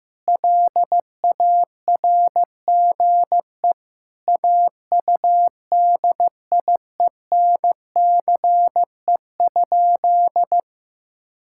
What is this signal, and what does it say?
Morse code, heard as a keyed tone, about 16 wpm, 700 Hz. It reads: LARGE AUDIENCE?
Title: Sentences from Top 1000 Words - Review Part 5 - 15wpm